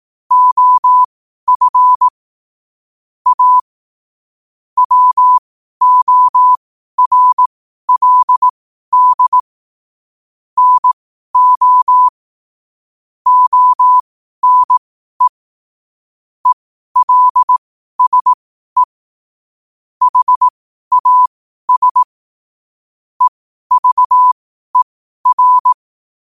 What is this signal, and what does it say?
Morse code, heard as a keyed tone, about 18 wpm, spaced as for 12 wpm, 1000 Hz. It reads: OF A WORLD NO ONE ELSE HAS EVER
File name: qotdfiles / 12wpm